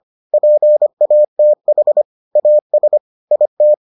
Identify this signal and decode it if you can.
Morse code, heard as a keyed tone, about 25 wpm, 600 Hz. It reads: PATH AS IT